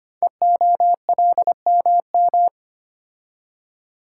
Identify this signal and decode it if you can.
Morse code, heard as a keyed tone, about 25 wpm, 700 Hz. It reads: EOLMM
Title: Code Groups - 5 Letters - 25wpm